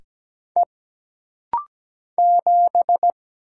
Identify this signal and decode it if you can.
Morse code, heard as a keyed tone, about 17 wpm, 700 Hz. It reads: E 7
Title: Single Letter-Number - Rapid-Fire 17wpm